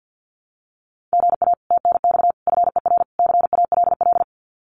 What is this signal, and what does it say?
Morse code, heard as a keyed tone, about 50 wpm, 700 Hz. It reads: GA TNX FER CALL
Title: Common QSO Phrases Encore 50wpm